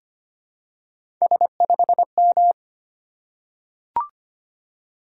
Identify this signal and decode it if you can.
Morse code, heard as a keyed tone, about 25 wpm, 700 Hz. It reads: S5M
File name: Three Letter-Number Combinations - 25wpm